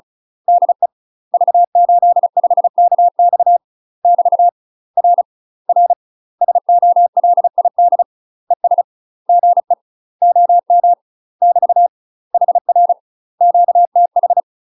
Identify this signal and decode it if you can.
Morse code, heard as a keyed tone, about 34 wpm, 700 Hz. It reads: DE V85KX <BT> R R SOLID ES GE OM <BT> HR QTH
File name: UK MASTER QSOs TRAINING/35WPM